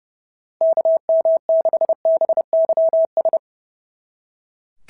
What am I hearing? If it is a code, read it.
Morse code, heard as a keyed tone, about 26 wpm, 650 Hz. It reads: KM6BYS
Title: SOTA QSO Practice 30wpm